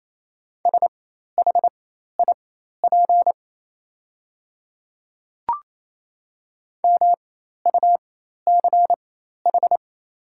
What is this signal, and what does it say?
Morse code, heard as a keyed tone, about 28 wpm, spaced as for 13 wpm, 700 Hz. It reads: SHIP MUCH
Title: 4x Character Spacing - Top 500 Words - 28wpm